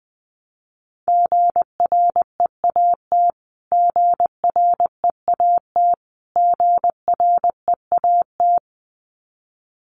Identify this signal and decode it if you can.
Morse code, heard as a keyed tone, about 20 wpm, 700 Hz. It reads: GREAT GREAT GREAT
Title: Top 200 Words Repeated 20wpm